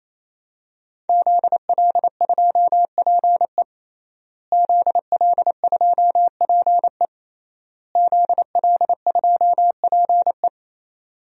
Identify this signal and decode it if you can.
Morse code, heard as a keyed tone, about 28 wpm, 700 Hz. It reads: ZL2PE ZL2PE ZL2PE
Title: DXCC-100 Call Signs - 5 Characters - Repeated 3x - 28wpm